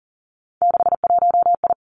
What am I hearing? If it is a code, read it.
Morse code, heard as a keyed tone, about 42 wpm, 700 Hz. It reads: 61I
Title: Three Challenging Letter-Number Combinations 40wpm